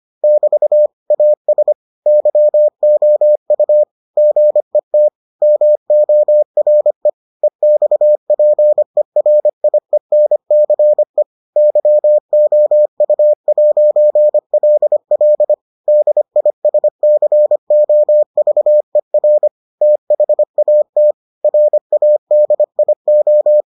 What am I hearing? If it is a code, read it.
Morse code, heard as a keyed tone, about 25 wpm, 600 Hz. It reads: <BT> AS YOU GET MORE EXPERIENCE YOU'LL DISCOVER THAT RADIO